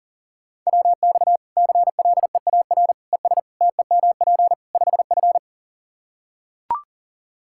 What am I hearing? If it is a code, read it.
Morse code, heard as a keyed tone, about 40 wpm, 700 Hz. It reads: WX CLEAR ES TEMP 5F
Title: Practice QSO - Exchange 2 - Info - 40wpm